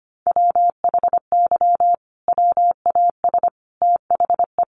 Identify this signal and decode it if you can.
Morse code, heard as a keyed tone, about 25 wpm, 700 Hz. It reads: WHY WAS THE
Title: Sentences from Top 1400 Words 25wpm